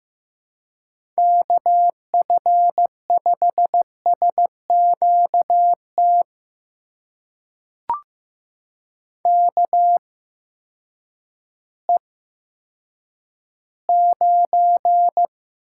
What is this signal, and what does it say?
Morse code, heard as a keyed tone, about 15 wpm, spaced as for 9 wpm, 700 Hz. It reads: KF5SQT KE9
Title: USA Call Signs - 8x Character Spacing - 15wpm